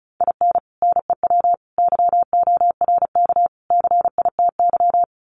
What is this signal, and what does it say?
Morse code, heard as a keyed tone, about 34 wpm, 700 Hz. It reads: IN NEW YORK CITY
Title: Sets of 4 Words 35wpm